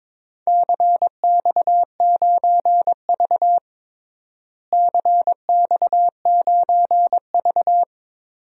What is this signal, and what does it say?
Morse code, heard as a keyed tone, about 22 wpm, 700 Hz. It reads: CX9V CX9V
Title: DXCC-100 Call Signs - 4 Characters - Repeated 2x - 22wpm